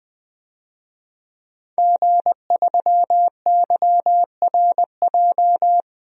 Morse code, heard as a keyed tone, about 20 wpm, 700 Hz. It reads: G3YRJ
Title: DXCC-100 Call Signs - 5 Characters - 20wpm